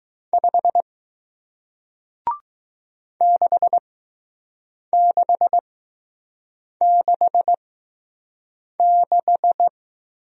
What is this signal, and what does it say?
Morse code, heard as a keyed tone, about 20 wpm, spaced as for 6 wpm, 700 Hz. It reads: 5 6666